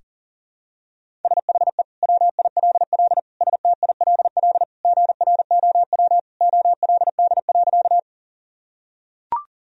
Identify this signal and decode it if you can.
Morse code, heard as a keyed tone, about 40 wpm, 700 Hz. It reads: SHE WILL STILL GROW OLD.
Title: Sentences from Top 1000 Words - Review Part 1 - 40wpm